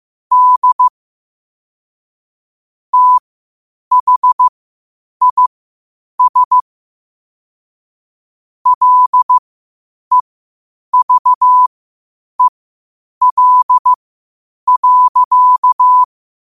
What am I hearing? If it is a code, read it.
Morse code, heard as a keyed tone, about 15 wpm, spaced as for 8 wpm, 1000 Hz. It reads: D THIS LEVEL.
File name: qotdfiles / 8wpm